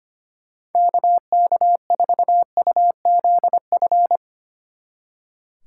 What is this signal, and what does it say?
Morse code, heard as a keyed tone, about 32 wpm, 700 Hz. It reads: KK4UZF